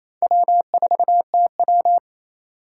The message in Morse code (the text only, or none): W4TW